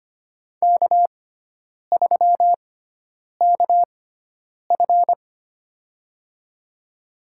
Morse code, heard as a keyed tone, about 25 wpm, spaced as for 9 wpm, 700 Hz. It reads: K3KF
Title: USA Call Signs - 6x Character Spacing - 25wpm